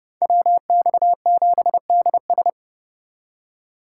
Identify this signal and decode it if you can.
Morse code, heard as a keyed tone, about 36 wpm, 700 Hz. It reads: WX7DS
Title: Speed Racing - USA 2x2 Call Signs 30wpm